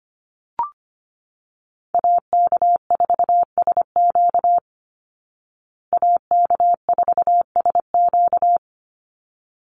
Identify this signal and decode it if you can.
Morse code, heard as a keyed tone, about 25 wpm, 700 Hz. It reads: AK4SQ AK4SQ
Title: Call Signs Repeated 3x - 25wpm